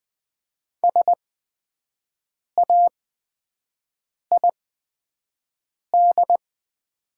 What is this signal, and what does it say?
Morse code, heard as a keyed tone, about 20 wpm, spaced as for 5 wpm, 700 Hz. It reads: SAID
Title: 8x Character Spacing - Top 100 Words - 20wpm